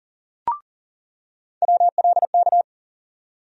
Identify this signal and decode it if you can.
Morse code, heard as a keyed tone, about 40 wpm, 700 Hz. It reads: WLK
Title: USA Call Sign Postfixes - 3 Letters 40wpm